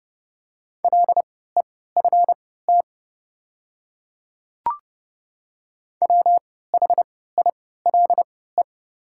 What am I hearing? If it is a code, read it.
Morse code, heard as a keyed tone, about 31 wpm, spaced as for 17 wpm, 700 Hz. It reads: LEFT WHILE
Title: 3x Character Spacing - Top 300 Words - 30wpm